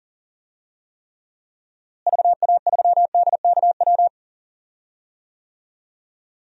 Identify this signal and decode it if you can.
Morse code, heard as a keyed tone, about 40 wpm, 700 Hz. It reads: VA3DKW
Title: DXCC-100 Call Signs - 6 Characters - Repeated 3x - 40wpm